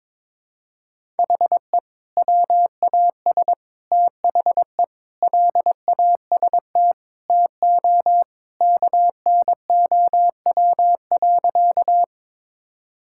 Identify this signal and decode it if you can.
Morse code, heard as a keyed tone, about 22 wpm, 700 Hz. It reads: HE WAS THE LAST TO KNOW.
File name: practice / 22wpm / Sentences with 2-Word Phrases - 22wpm